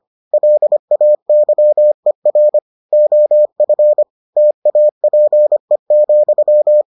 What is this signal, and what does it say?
Morse code, heard as a keyed tone, about 25 wpm, 600 Hz. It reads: LAYER OF TAPE,